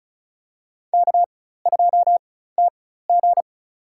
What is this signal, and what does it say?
Morse code, heard as a keyed tone, about 34 wpm, spaced as for 16 wpm, 700 Hz. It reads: K2TG
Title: USA Call Signs - 4x Character Spacing - 35wpm